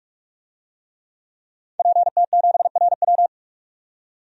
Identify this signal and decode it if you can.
Morse code, heard as a keyed tone, about 46 wpm, 700 Hz. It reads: PT7RW
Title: DXCC-100 Call Signs - 5 Characters - 45wpm